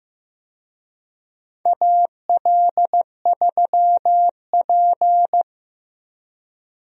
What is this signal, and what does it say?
Morse code, heard as a keyed tone, about 15 wpm, 700 Hz. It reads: AL3P
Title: USA Call Signs - 4 Characters - Repeated 3x - 15wpm